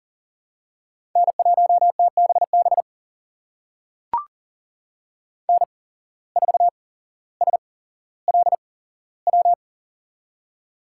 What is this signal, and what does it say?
Morse code, heard as a keyed tone, about 40 wpm, spaced as for 17 wpm, 700 Hz. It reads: N1TBB N4SLW
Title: USA Call Signs - 5 Characters - 8x Character Spacing - 40wpm